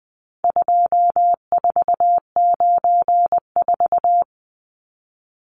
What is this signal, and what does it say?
Morse code, heard as a keyed tone, about 20 wpm, 700 Hz. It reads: 2494
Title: Numbers - 4-Digit 20wpm